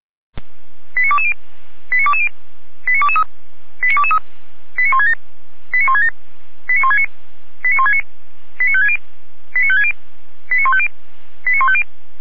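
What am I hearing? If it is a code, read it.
Five-tone selective call (ZVEI), heard as quick runs of five tones, tones of about 70 ms. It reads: 89220 89220 89223 89223 89167 89167 89169 89169 89560 89560 89150 89150
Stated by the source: ZVEI five-tone selective call received on an SDR.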